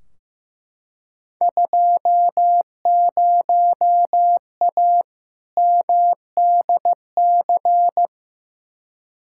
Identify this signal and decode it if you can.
Morse code, heard as a keyed tone, about 15 wpm, 700 Hz. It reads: 20A MDC
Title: Field Day Exchanges Encore - 15wpm